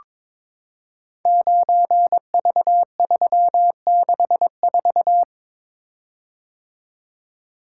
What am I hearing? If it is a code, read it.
Morse code, heard as a keyed tone, about 22 wpm, 700 Hz. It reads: 9V364
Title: Code Groups - 5 Letter:Numbers - 22wpm